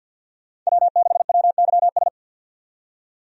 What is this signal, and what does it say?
Morse code, heard as a keyed tone, about 50 wpm, 700 Hz. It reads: W6WYS
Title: USA Call Signs - 5 Characters - 50wpm